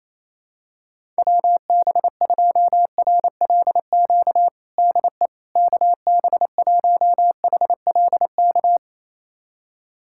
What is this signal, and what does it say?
Morse code, heard as a keyed tone, about 28 wpm, 700 Hz. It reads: WB2RLQ DE KB1HLK
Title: USA Callsign DE Callsign - 6 Characters - 28wpm